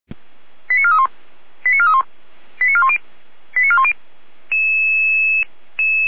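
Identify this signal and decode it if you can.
Five-tone selective call (ZVEI), heard as quick runs of five tones, tones of about 70 ms. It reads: 89431 89431 89410 89410
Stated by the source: ZVEI five-tone call received by SDR.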